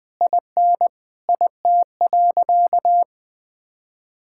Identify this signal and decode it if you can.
Morse code, heard as a keyed tone, about 20 wpm, 700 Hz. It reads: IN IT.
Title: Sentences from Top 1600 Words 20wpm